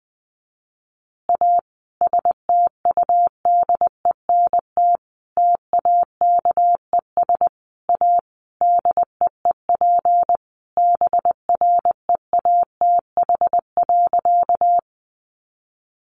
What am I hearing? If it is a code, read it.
Morse code, heard as a keyed tone, about 20 wpm, 700 Hz. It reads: A STUDENT TAKES A DEEP BREATH.
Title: Additional Easy Phrases 20wpm